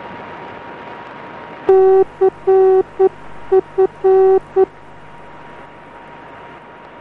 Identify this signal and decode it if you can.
Morse code, heard as a keyed tone, about 9 wpm, 380 Hz. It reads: CF